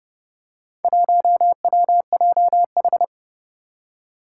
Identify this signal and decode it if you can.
Morse code, heard as a keyed tone, about 30 wpm, 700 Hz. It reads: 1WJH
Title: Four Challenging Letter-Number Combinations 30wpm